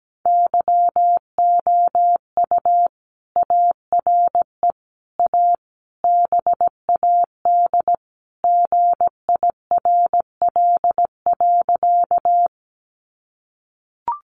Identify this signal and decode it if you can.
Morse code, heard as a keyed tone, about 17 wpm, 700 Hz. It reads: YOU ARE A BAD GIRL.
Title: Sentences from Top 700 Words 17wpm